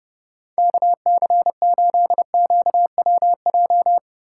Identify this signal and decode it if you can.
Morse code, heard as a keyed tone, about 37 wpm, 700 Hz. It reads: KC8QWJ
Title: Speed Racing - USA 2x3 Call Signs 30wpm